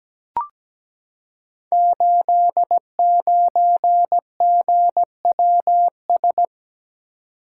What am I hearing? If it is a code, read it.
Morse code, heard as a keyed tone, about 17 wpm, 700 Hz. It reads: 89GWS
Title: Code Groups - 5 Letter:Numbers - 17wpm